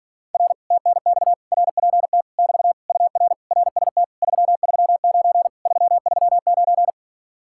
Morse code, heard as a keyed tone, about 58 wpm, 700 Hz. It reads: R TNX RPT <BT> UR RST 339 339